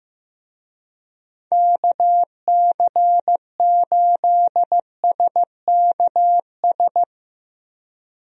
Morse code, heard as a keyed tone, about 15 wpm, 700 Hz. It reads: KC8SKS
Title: USA Call Signs - 6 Characters - Repeated 3x - 15wpm